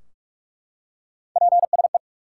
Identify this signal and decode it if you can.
Morse code, heard as a keyed tone, about 44 wpm, 700 Hz. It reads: PSE
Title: Top 90 QSO Elements 45wpm